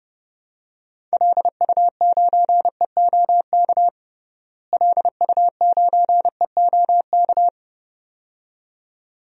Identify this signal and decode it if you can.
Morse code, heard as a keyed tone, about 30 wpm, 700 Hz. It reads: LU9EOK LU9EOK
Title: DXCC-100 Call Signs - 6 Characters - Repeated 2x - 30wpm